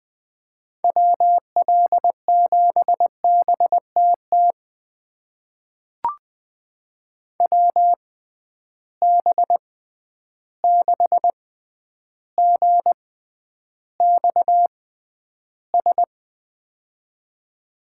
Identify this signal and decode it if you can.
Morse code, heard as a keyed tone, about 20 wpm, spaced as for 10 wpm, 700 Hz. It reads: WL7BTT WB6GXS
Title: USA Call Signs - 6 Characters - 6x Character Spacing - 20wpm